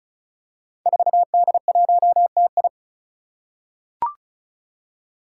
Morse code, heard as a keyed tone, about 36 wpm, 700 Hz. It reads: 4D1TI
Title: Five Challenging Letter-Number Combinations 35wpm